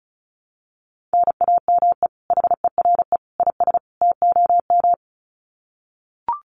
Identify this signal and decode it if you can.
Morse code, heard as a keyed tone, about 34 wpm, 700 Hz. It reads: NAME HERE IS TOM
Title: Common QSO Phrases 35wpm